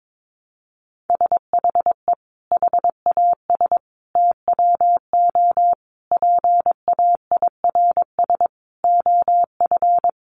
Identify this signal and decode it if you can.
Morse code, heard as a keyed tone, about 22 wpm, 700 Hz. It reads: SHE HAS TWO PAIRS OF
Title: Additional Easy Phrases 22wpm